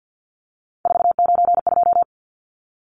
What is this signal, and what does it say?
Morse code, heard as a keyed tone, about 50 wpm, 700 Hz. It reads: 492
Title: Numbers - 3-Digit 50wpm